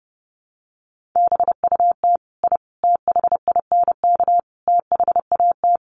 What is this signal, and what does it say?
Morse code, heard as a keyed tone, about 30 wpm, 700 Hz. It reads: BUT I THINK THAT
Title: Sets of 4 Words 30wpm